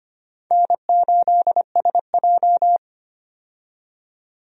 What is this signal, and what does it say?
Morse code, heard as a keyed tone, about 25 wpm, 700 Hz. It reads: N8SJ
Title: USA Call Signs - 4 Characters - 25wpm